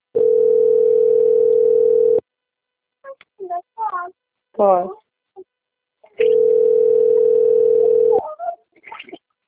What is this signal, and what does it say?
Ringback tone (North America), 440 + 480 Hz, with speech.
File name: out- - - - - .50054